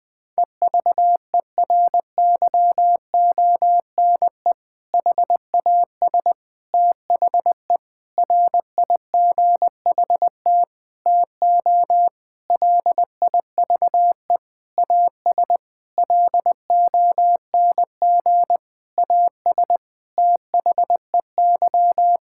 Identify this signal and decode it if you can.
Morse code, heard as a keyed tone, about 20 wpm, 700 Hz. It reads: EVERYONE HAS THE RIGHT TO LIVE AS LONG AS THEY
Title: Sentences from Top 1000 Words - Review Part 5 - 20wpm